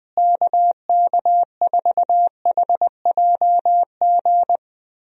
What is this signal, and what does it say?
Morse code, heard as a keyed tone, about 21 wpm, 700 Hz. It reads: KK4HJG
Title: USA Call Signs - 6 Characters - Repeated 3x - 20wpm